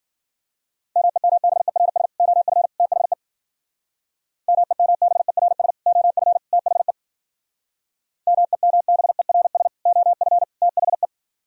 Morse code, heard as a keyed tone, about 52 wpm, 700 Hz. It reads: MEMBERS OF THE MEMBERS OF THE MEMBERS OF THE